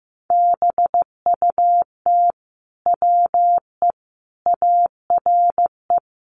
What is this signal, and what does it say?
Morse code, heard as a keyed tone, about 15 wpm, 700 Hz. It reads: BUT WE ARE
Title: Sentences from Top 800 Words 15wpm